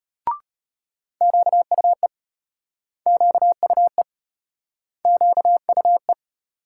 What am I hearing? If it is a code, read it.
Morse code, heard as a keyed tone, about 32 wpm, 700 Hz. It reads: QUE QUE QUE